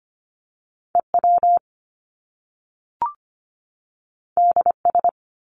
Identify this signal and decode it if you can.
Morse code, heard as a keyed tone, about 25 wpm, 700 Hz. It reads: EW DS
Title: Top Two Letter Combinations in Words 25wpm